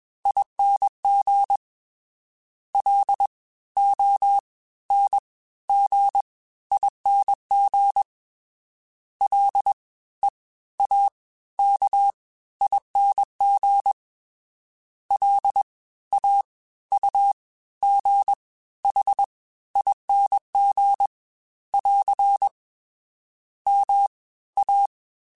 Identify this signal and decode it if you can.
Morse code, heard as a keyed tone, about 21 wpm, spaced as for 14 wpm, 800 Hz. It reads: ING LONGING LEAKING LAUGHING <AR> MA